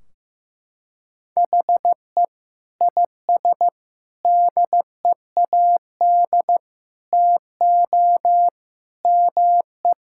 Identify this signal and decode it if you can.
Morse code, heard as a keyed tone, about 15 wpm, 700 Hz. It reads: HE IS DEAD TO ME
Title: Sentences from Top 1600 Words 15wpm